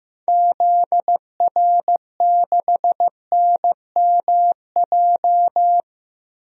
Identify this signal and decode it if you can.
Morse code, heard as a keyed tone, about 15 wpm, 700 Hz. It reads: ZR6NMJ
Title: DXCC-100 Call Signs - 6 Characters - 15wpm